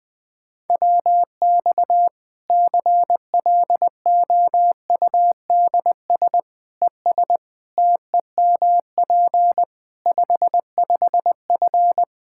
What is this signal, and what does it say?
Morse code, heard as a keyed tone, about 20 wpm, 700 Hz. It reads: WX CLOUDS ES TEMP 55F
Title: K1USN SST QSO Practice 20wpm